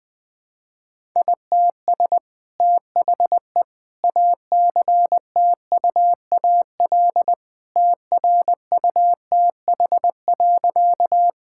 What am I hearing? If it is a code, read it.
Morse code, heard as a keyed tone, about 20 wpm, 700 Hz. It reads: ITS THE ACTUAL TRUTH.
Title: Sentences from Top 1000 Words - Review Part 5 - 20wpm